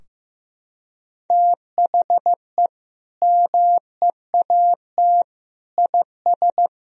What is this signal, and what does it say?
Morse code, heard as a keyed tone, about 15 wpm, 700 Hz. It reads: THE MEAT IS
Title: Sentences from Top 1000 Words - Review Part 3 - 15wpm